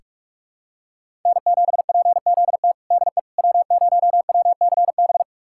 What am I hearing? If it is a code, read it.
Morse code, heard as a keyed tone, about 46 wpm, 700 Hz. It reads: N7PZT DE W0WCB
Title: USA Callsign DE Callsign 45wpm